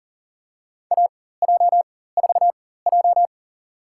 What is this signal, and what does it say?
Morse code, heard as a keyed tone, about 40 wpm, spaced as for 19 wpm, 700 Hz. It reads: AJ4J
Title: USA Call Signs - 4 Characters - 4x Character Spacing - 40wpm